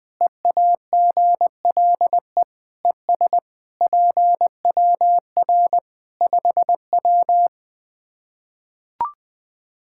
Morse code, heard as a keyed tone, about 20 wpm, 700 Hz. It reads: EAGLE ES PWR 5W
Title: Practice QSO - Exchange 2 - Info - 20wpm